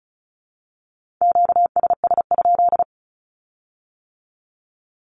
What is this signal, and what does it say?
Morse code, heard as a keyed tone, about 36 wpm, 700 Hz. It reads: QSS?
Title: Q-Codes with Spoken Meaning - 35wpm